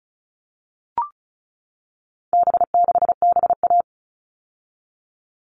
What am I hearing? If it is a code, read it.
Morse code, heard as a keyed tone, about 34 wpm, 700 Hz. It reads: B6BA
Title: All Cut Numbers - 4-Digit 35wpm